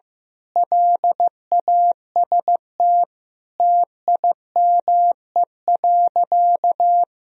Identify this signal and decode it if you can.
Morse code, heard as a keyed tone, about 15 wpm, 700 Hz. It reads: LAST TIME.
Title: Sentences from Top 1000 Words - Review Part 2 - 15wpm